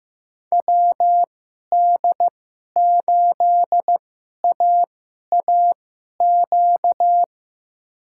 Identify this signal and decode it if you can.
Morse code, heard as a keyed tone, about 15 wpm, spaced as for 11 wpm, 700 Hz. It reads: WD8AAQ
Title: USA Call Signs - 6 Characters - 2x Character Spacing - 15wpm